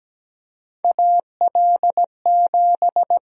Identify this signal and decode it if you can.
Morse code, heard as a keyed tone, about 17 wpm, 700 Hz. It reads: AL7
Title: USA Call Sign Prefixes - 17wpm